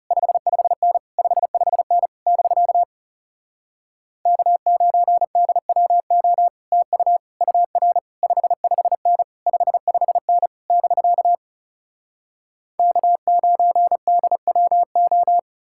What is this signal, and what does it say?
Morse code, heard as a keyed tone, about 40 wpm, 700 Hz. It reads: 55N 55N <BK> K9DWO TU UR 55N 55N <BK> K9DWO